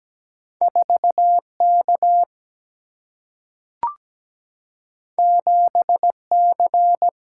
Morse code, heard as a keyed tone, about 17 wpm, 700 Hz. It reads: 4K 7C
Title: Single-Letter Single-Number 17wpm